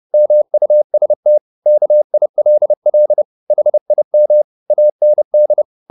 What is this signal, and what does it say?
Morse code, heard as a keyed tone, about 30 wpm, 600 Hz. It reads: MUST KILL HIM AND
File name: the old man and the sea / the old man and the sea 1 30wpm